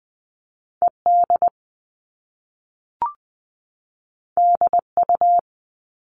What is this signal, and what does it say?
Morse code, heard as a keyed tone, about 20 wpm, 700 Hz. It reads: ED DU
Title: All Cut Numbers - 2-Digit 20wpm